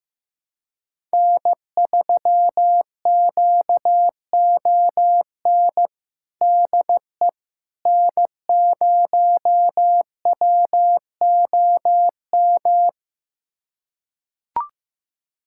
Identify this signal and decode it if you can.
Morse code, heard as a keyed tone, about 15 wpm, 700 Hz. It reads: N3QON DE N0WOM
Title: USA Callsign DE Callsign - 5 Characters - 15wpm